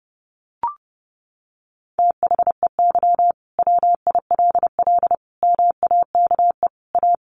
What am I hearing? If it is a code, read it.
Morse code, heard as a keyed tone, about 30 wpm, 700 Hz. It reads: THEY WILL MAKE A
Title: Sentences from Top 1500 Words 30wpm